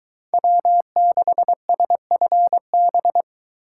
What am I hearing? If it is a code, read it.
Morse code, heard as a keyed tone, about 23 wpm, 700 Hz. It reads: W6SFB